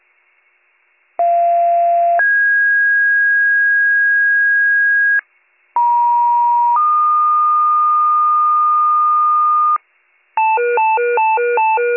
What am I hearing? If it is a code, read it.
Two-tone page, heard as a one-second tone, then a three-second tone, about 690 then 1650 Hz, about 950 then 1220 Hz, with noise. It is heard from 1.2 to 9.8 s.